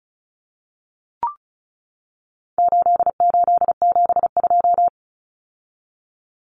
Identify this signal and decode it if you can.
Morse code, heard as a keyed tone, about 34 wpm, 700 Hz. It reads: 8872
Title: Numbers - 4-Digit 35wpm